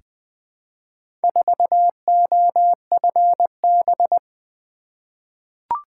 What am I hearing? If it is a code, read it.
Morse code, heard as a keyed tone, about 20 wpm, 700 Hz. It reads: 4OFB
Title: Four Letter-Number Combinations - 20wpm